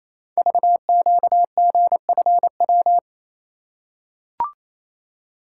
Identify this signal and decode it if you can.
Morse code, heard as a keyed tone, about 27 wpm, 700 Hz. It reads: VQGFW
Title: Code Groups - 5 Letters - 28wpm